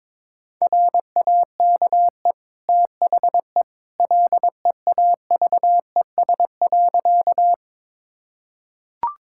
Morse code, heard as a keyed tone, about 22 wpm, 700 Hz. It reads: RAKE THE LEAVES.
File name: practice / 22wpm / Easy Sentences with Hints - 22wpm